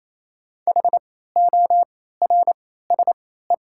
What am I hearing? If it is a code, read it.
Morse code, heard as a keyed tone, about 28 wpm, spaced as for 16 wpm, 700 Hz. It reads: HORSE
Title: 3x Character Spacing - Top 300 Words - 28wpm